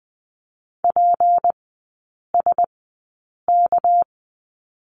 Morse code, heard as a keyed tone, about 20 wpm, spaced as for 8 wpm, 700 Hz. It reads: PSK
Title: 198 Lesson 32e - Letter k - Three Characters 20wpm